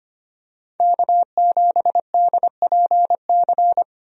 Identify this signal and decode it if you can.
Morse code, heard as a keyed tone, about 25 wpm, 700 Hz. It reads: K7DPC
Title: USA Call Signs - 5 Characters - Repeated 2x - 25wpm